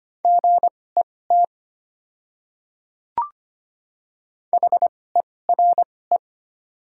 Morse code, heard as a keyed tone, about 25 wpm, spaced as for 18 wpm, 700 Hz. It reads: GET HERE
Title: 2x Character Spacing - Top 200 Words - 25wpm